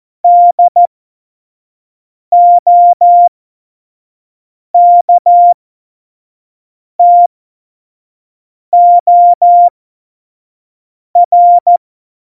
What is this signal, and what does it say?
Morse code, heard as a keyed tone, about 14 wpm, spaced as for 5 wpm, 700 Hz. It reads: DOKTOR
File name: Morsekurs / Witze / Witz 27 March 2018, 25 F70